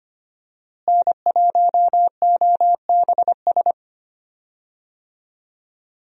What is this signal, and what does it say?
Morse code, heard as a keyed tone, about 25 wpm, 700 Hz. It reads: N1OBS
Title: USA Call Signs - 5 Characters - Repeated 3x - 25wpm